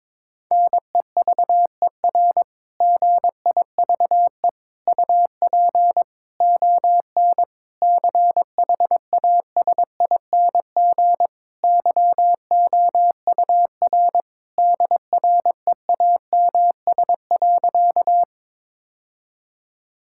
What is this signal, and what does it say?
Morse code, heard as a keyed tone, about 22 wpm, 700 Hz. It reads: NEVER GIVE UP ON CHASING YOUR DREAMS.
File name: Difficult Sentences with Hints - 22wpm